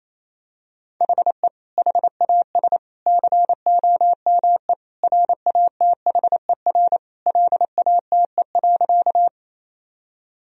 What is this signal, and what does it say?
Morse code, heard as a keyed tone, about 28 wpm, 700 Hz. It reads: HE HAS COME RATHER LATE.
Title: Sentences from Top 1000 Words - Review Part 4 - 28wpm